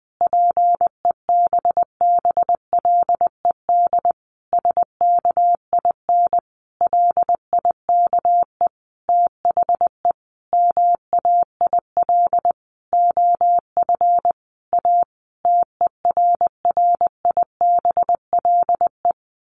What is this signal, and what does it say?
Morse code, heard as a keyed tone, about 20 wpm, 700 Hz. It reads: PEBBLED SKIN LIKE THE MAIL OF A TERRIBLE
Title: A Sound Of Thunder 20wpm